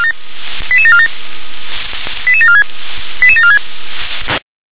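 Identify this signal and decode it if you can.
Five-tone selective call (ZVEI), heard as quick runs of five tones, tones of about 70 ms. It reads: 88747 88746 88746
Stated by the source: ZVEI five-tone call 88603 received on an SDR.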